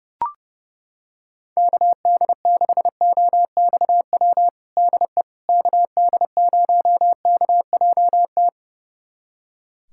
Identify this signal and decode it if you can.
Morse code, heard as a keyed tone, about 30 wpm, 700 Hz. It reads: KD6OXW DE KD0KJT
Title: USA Callsign DE Callsign - 6 Characters - 30wpm